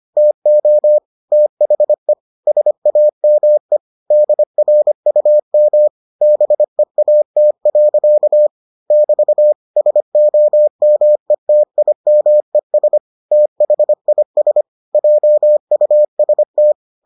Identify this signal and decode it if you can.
Morse code, heard as a keyed tone, about 25 wpm, 600 Hz. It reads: TO THE SAME DRUM BEAT. <BT> SOMETIMES THIS JUST